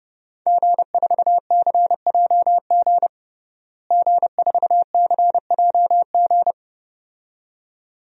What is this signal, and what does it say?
Morse code, heard as a keyed tone, about 30 wpm, 700 Hz. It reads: G4CJG G4CJG
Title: DXCC-100 Call Signs - 5 Characters - Repeated 2x - 30wpm